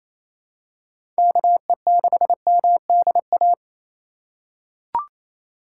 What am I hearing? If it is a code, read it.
Morse code, heard as a keyed tone, about 27 wpm, 700 Hz. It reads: KE6MDA
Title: USA Call Signs - 6 Characters - 3x Character Spacing - 28wpm